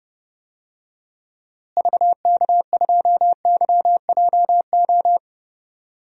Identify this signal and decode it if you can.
Morse code, heard as a keyed tone, about 28 wpm, 700 Hz. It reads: VK2YJO
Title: DXCC-100 Call Signs - 6 Characters - Repeated 2x - 30wpm